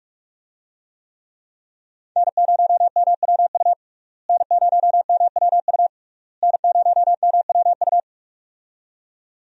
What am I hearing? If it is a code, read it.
Morse code, heard as a keyed tone, about 46 wpm, 700 Hz. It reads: N0MWU N0MWU N0MWU
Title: USA Call Signs - 5 Characters - Repeated 3x - 45wpm